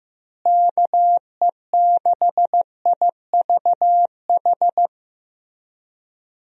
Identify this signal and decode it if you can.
Morse code, heard as a keyed tone, about 15 wpm, 700 Hz. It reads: KE6IVH
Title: USA Call Signs - 6 Characters - Repeated 2x - 15wpm